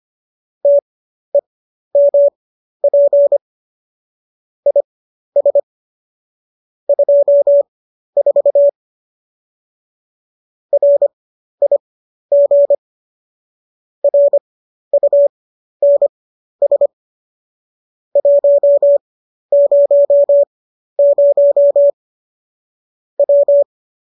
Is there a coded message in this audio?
Yes: Morse code from 0.6 to 23.6 s, heard as a keyed tone, about 25 wpm, spaced as for 12 wpm, 570 Hz. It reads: TEMP IS 24 RIG RUNS 100 W